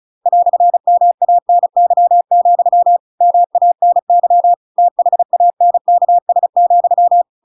Morse code, heard as a keyed tone, about 34 wpm, 700 Hz. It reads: "MANY, MANY THANKS,